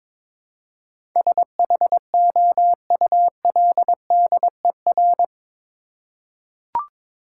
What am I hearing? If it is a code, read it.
Morse code, heard as a keyed tone, about 22 wpm, 700 Hz. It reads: SHOULDER
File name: practice / 22wpm / Eight Letter Words - 22wpm